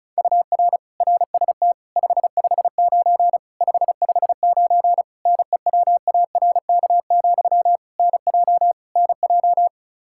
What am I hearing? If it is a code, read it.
Morse code, heard as a keyed tone, about 36 wpm, 700 Hz. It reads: UR RST 559 559 NEWARK, NJ NJ
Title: Practice QSO - Exchange 1 - Beginning - 35wpm